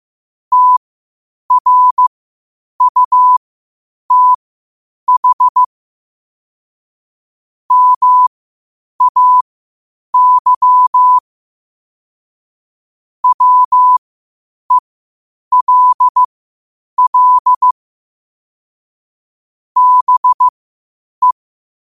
Morse code, heard as a keyed tone, about 15 wpm, spaced as for 8 wpm, 1000 Hz. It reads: TRUTH MAY WELL BE